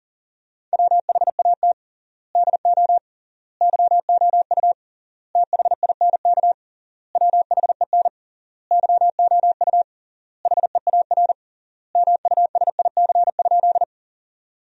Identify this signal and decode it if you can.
Morse code, heard as a keyed tone, about 40 wpm, 700 Hz. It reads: WHAT DO YOU THINK WHEN YOU HEAR MUSIC?
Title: 3x Word-Spacing - Sentences from Top 300 Words - 40wpm